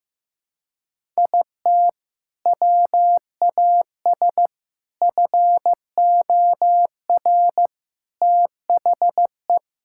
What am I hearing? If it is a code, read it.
Morse code, heard as a keyed tone, about 15 wpm, 700 Hz. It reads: IT WAS FOR THE
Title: Sentences from Top 1000 Words - Review Part 6 - 15wpm